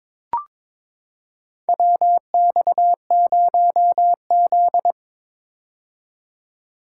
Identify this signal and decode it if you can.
Morse code, heard as a keyed tone, about 22 wpm, 700 Hz. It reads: WX0Z
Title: K1USN SST QSO Practice - 22wpm